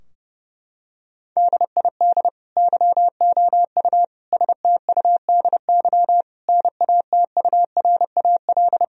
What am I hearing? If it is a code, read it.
Morse code, heard as a keyed tone, about 30 wpm, 700 Hz. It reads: DID YOU STUDY NATURAL
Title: Sentences from Top 1000 Words - Review Part 4 - 30wpm